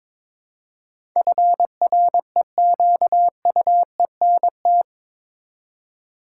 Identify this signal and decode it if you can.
Morse code, heard as a keyed tone, about 22 wpm, 700 Hz. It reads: FREQUENT
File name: practice / 22wpm / Eight Letter Words - 22wpm